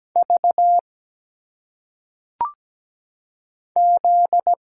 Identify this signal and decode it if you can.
Morse code, heard as a keyed tone, about 17 wpm, 700 Hz. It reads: V Z
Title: Single Letters - Phonetic Pronunciation 17wpm